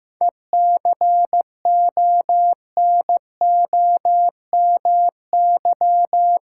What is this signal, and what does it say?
Morse code, heard as a keyed tone, about 15 wpm, 700 Hz. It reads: ECONOMY
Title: Seven Letter Words - 15wpm